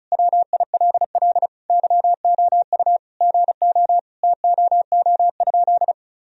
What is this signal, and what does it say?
Morse code, heard as a keyed tone, about 36 wpm, 700 Hz. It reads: WILL YOU GO TOO?
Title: Sentences from Top 1000 Words - Review Part 2 - 35wpm